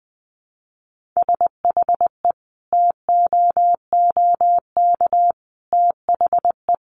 Morse code, heard as a keyed tone, about 20 wpm, 700 Hz. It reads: SHE TOOK THE
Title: Sentences from Top 1400 Words 20wpm